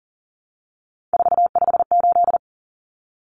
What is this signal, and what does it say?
Morse code, heard as a keyed tone, about 40 wpm, 700 Hz. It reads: 458
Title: Common Cut Numbers - 3-Digit 40wpm